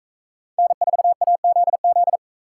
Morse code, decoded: N4AZZ

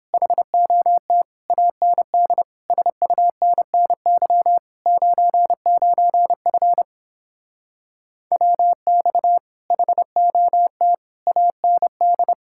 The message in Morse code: HOT AND SUNNY 99F WX HOT AND

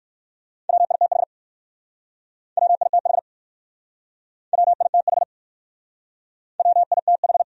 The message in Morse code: WITH WITH WITH WITH

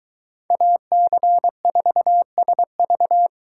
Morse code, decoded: AC4SV

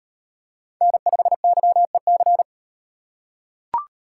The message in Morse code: N5YEC

